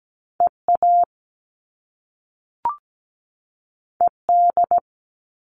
Morse code, heard as a keyed tone, about 17 wpm, 700 Hz. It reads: EA ED
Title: All Cut Numbers - 2-Digit 17wpm